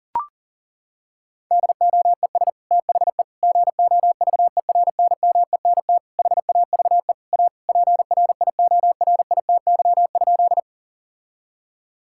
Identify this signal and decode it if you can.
Morse code, heard as a keyed tone, about 40 wpm, 700 Hz. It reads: DOES THE GOVERNMENT HAVE A PRIORITY?
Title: Sentences from Top 1600 Words 40wpm